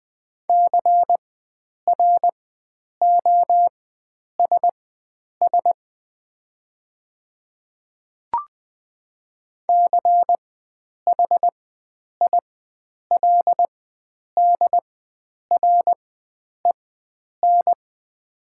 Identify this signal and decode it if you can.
Morse code, heard as a keyed tone, about 20 wpm, spaced as for 9 wpm, 700 Hz. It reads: CROSS CHILDREN
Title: 4x Character Spacing - Top 500 Words - 20wpm